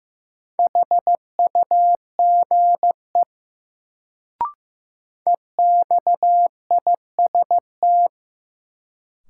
Morse code, heard as a keyed tone, about 15 wpm, 700 Hz. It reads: HUGE EXIST
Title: Sentences from Top 1100 Words - New Word Drill Alt - 15wpm